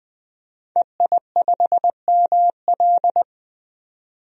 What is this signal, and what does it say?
Morse code, heard as a keyed tone, about 20 wpm, 700 Hz. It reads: EI5ML
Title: DXCC-100 Call Signs - 5 Characters - 20wpm